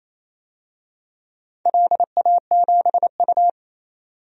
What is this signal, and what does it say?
Morse code, heard as a keyed tone, about 28 wpm, 700 Hz. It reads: LA7U
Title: DXCC-100 Call Signs - 4 Characters - 28wpm